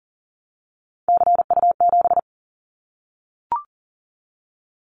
Morse code, heard as a keyed tone, about 40 wpm, 700 Hz. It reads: CU7
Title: Callsign Prefixes for Top 50 DXCC - 40wpm